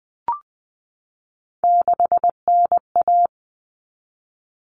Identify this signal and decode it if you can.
Morse code, heard as a keyed tone, about 20 wpm, 700 Hz. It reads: 6NA
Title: Common Cut Numbers - 3-Digit 20wpm